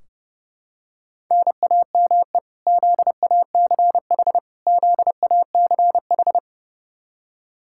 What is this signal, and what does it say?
Morse code, heard as a keyed tone, about 30 wpm, 700 Hz. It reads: NAME ZACH ZACH
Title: Practice QSO - Exchange 1 - Beginning - 30wpm